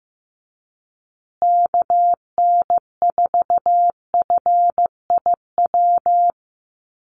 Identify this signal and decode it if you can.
Morse code, heard as a keyed tone, about 15 wpm, 700 Hz. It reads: KN4FIW
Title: USA Call Signs Repeated 2x 15wpm